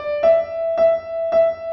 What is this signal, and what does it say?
There is no code.